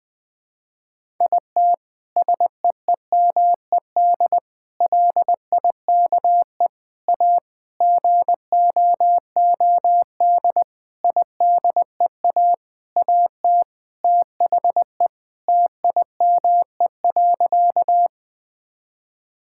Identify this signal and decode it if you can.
Morse code, heard as a keyed tone, about 20 wpm, 700 Hz. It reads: IT SEEMED LIKE A GOOD IDEA AT THE TIME.
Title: Sentences with 9-Word Phrases - 20wpm